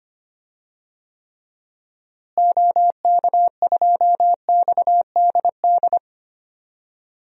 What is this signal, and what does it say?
Morse code, heard as a keyed tone, about 25 wpm, 700 Hz. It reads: OK2XDD
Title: DXCC-100 Call Signs - 6 Characters - Repeated 2x - 25wpm